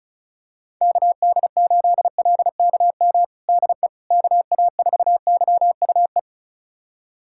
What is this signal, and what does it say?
Morse code, heard as a keyed tone, about 34 wpm, 700 Hz. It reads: KD8LKM DE KA4YUE
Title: USA Callsign DE Callsign - 6 Characters - 35wpm